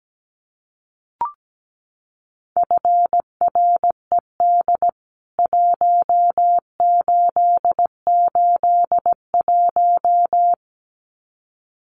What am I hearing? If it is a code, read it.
Morse code, heard as a keyed tone, about 17 wpm, 700 Hz. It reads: FRED 1881
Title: CWT Exchange 17wpm